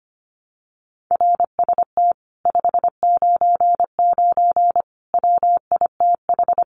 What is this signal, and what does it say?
Morse code, heard as a keyed tone, about 25 wpm, 700 Hz. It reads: RST 599 WITH
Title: Common QSO Phrases Encore 25wpm